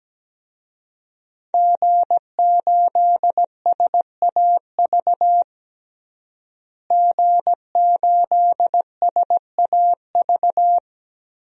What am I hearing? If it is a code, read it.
Morse code, heard as a keyed tone, about 17 wpm, 700 Hz. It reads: G8SAV G8SAV
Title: DXCC-100 Call Signs - 5 Characters - Repeated 2x - 17wpm